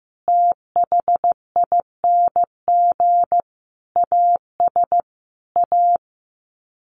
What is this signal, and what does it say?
Morse code, heard as a keyed tone, about 15 wpm, 700 Hz. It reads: THING AS A
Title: Sets of 5 Words Encore 15wpm